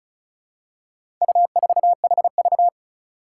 Morse code, heard as a keyed tone, about 36 wpm, 700 Hz. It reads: U4HV
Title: Four Challenging Letter-Number Combinations 35wpm